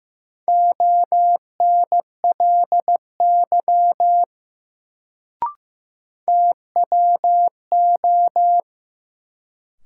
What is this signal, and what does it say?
Morse code, heard as a keyed tone, about 15 wpm, 700 Hz. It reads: ONLY TWO
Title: Sets of 2 Words - New Word Drill - 15wpm